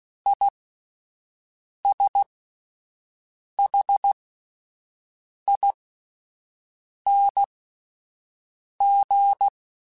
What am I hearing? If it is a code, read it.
Morse code, heard as a keyed tone, about 16 wpm, spaced as for 5 wpm, 800 Hz. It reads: ISHING
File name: Combinations / Combinations-mp3s / Combinations@5